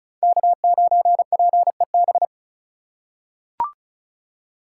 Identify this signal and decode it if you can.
Morse code, heard as a keyed tone, about 34 wpm, 700 Hz. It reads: K9PEB
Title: USA Call Signs - 5 Characters - 2x Character Spacing - 35wpm